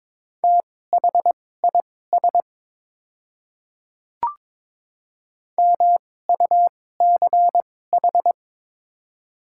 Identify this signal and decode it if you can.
Morse code, heard as a keyed tone, about 22 wpm, spaced as for 16 wpm, 700 Hz. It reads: THIS MUCH
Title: 2x Character Spacing - Top 200 Words - 22wpm